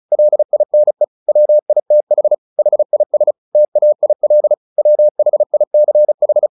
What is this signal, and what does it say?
Morse code, heard as a keyed tone, about 34 wpm, 600 Hz. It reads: LINE WITH HIS TAIL WHICH
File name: the old man and the sea / the old man and the sea 1 35wpm